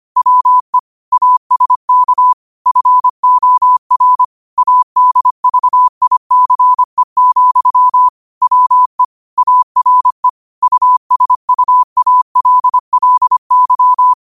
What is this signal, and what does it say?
Morse code, heard as a keyed tone, about 25 wpm, 1000 Hz. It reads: WE ASK FOR ADVICE, WE ARE USUALLY
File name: qotdfiles / 25wpm